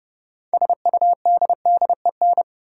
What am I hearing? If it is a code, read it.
Morse code, heard as a keyed tone, about 30 wpm, 700 Hz. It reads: SUDDEN
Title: Sets of 4 Words - New Word Drill - 30wpm